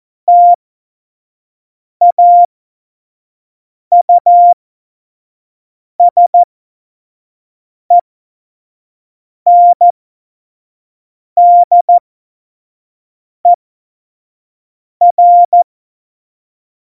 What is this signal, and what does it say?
Morse code, heard as a keyed tone, about 14 wpm, spaced as for 5 wpm, 700 Hz. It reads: TAUSENDER
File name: Morsekurs / Witze / Witz 25 Juli 17, Tempo 25 F70